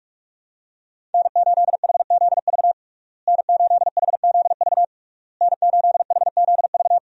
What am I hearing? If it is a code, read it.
Morse code, heard as a keyed tone, about 41 wpm, 700 Hz. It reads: N8HZV N8HZV N8HZV
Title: USA Call Signs - 5 Characters - Repeated 3x - 45wpm